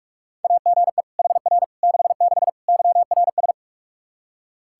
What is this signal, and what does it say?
Morse code, heard as a keyed tone, about 41 wpm, 700 Hz. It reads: AGE HR 66 YRS
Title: Practice QSO - Exchange 3 - More Info - 45wpm